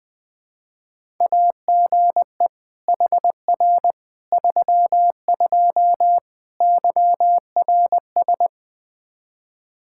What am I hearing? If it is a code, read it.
Morse code, heard as a keyed tone, about 20 wpm, 700 Hz. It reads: AGE HR 32 YRS
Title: Practice QSO - Exchange 3 - More Info - 20wpm